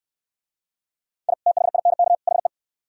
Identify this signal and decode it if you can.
Morse code, heard as a keyed tone, about 65 wpm, 700 Hz. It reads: I THINK HE